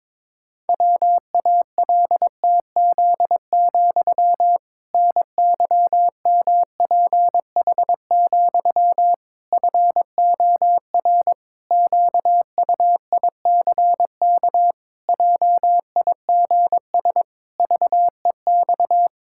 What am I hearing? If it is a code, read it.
Morse code, heard as a keyed tone, about 22 wpm, 700 Hz. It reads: WALTZ, NYMPH, FOR QUICK JIGS VEX
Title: Pangram Challenge v2 - 22wpm